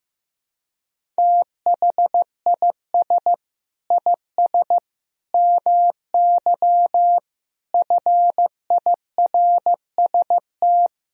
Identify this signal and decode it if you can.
Morse code, heard as a keyed tone, about 15 wpm, 700 Hz. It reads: THIS IS MY FIRST
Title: Sentences from Top 1000 Words - Review Part 3 - 15wpm